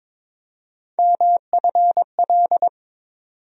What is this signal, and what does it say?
Morse code, heard as a keyed tone, about 21 wpm, 700 Hz. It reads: MFL
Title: USA Call Sign Postfixes - 3 Letters 22wpm